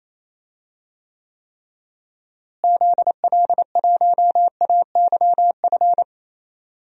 Morse code, heard as a keyed tone, about 27 wpm, 700 Hz. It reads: ZL1AYF
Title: DXCC-100 Call Signs - 6 Characters - Repeated 3x - 28wpm